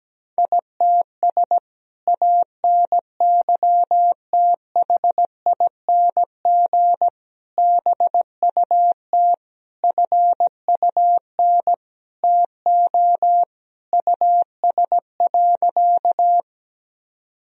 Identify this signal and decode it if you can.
Morse code, heard as a keyed tone, about 17 wpm, 700 Hz. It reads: ITS ANYTHING BUT FUN TO US.
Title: Sentences from Top 1000 Words - Review Part 4 - 17wpm